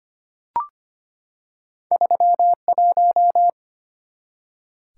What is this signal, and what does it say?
Morse code, heard as a keyed tone, about 25 wpm, 700 Hz. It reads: 31
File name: Two Letter-Number Combinations - 25wpm